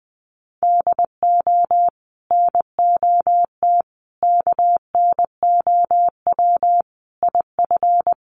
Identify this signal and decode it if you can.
Morse code, heard as a keyed tone, about 20 wpm, 700 Hz. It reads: DO NOT KNOW IF
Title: Sets of 5 Words Encore2 20wpm